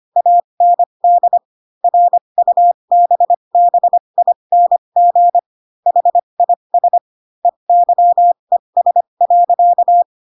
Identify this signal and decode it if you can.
Morse code, heard as a keyed tone, about 25 wpm, spaced as for 22 wpm, 700 Hz. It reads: AND RUBBING HIS EYES.
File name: Pinocchio / Pinocchio ch6 18 25 ik1hge